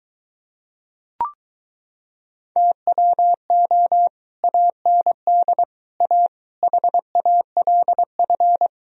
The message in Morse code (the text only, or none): TWO AND A HALF